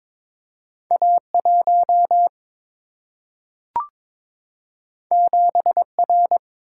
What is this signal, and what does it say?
Morse code, heard as a keyed tone, about 22 wpm, 700 Hz. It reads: A1 7R
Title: Single-Letter Single-Number 22wpm